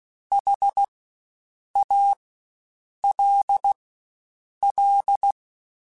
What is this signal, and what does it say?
Morse code, heard as a keyed tone, about 16 wpm, spaced as for 7 wpm, 800 Hz. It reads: HALL
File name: Spaced Alphas / Spaced Alphas mp3s / Hs sp @15